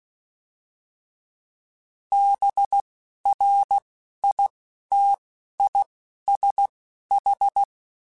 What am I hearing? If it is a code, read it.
Morse code, heard as a keyed tone, about 16 wpm, spaced as for 11 wpm, 800 Hz. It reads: BRITISH
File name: General / General-mp3s / ISH @10